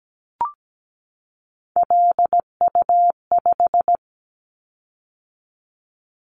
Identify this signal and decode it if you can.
Morse code, heard as a keyed tone, about 17 wpm, 700 Hz. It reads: LU5
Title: Callsign Prefixes for Top 50 DXCC - 17wpm